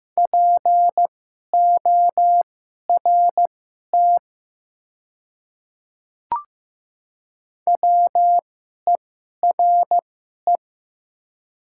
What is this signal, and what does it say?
Morse code, heard as a keyed tone, about 15 wpm, spaced as for 11 wpm, 700 Hz. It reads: PORT WERE
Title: 2x Character Spacing - Top 300 Words - 15wpm